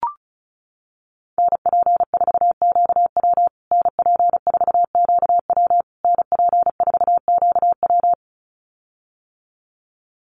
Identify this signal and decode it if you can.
Morse code, heard as a keyed tone, about 34 wpm, 700 Hz. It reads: NP4QW NP4QW NP4QW
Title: USA Call Signs Repeated 3x 35wpm